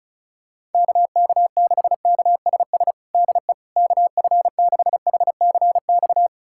Morse code, heard as a keyed tone, about 36 wpm, 690 Hz. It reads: KK6KSS DE KF6HCX
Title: USA Callsign DE Callsign - 6 Characters - 35wpm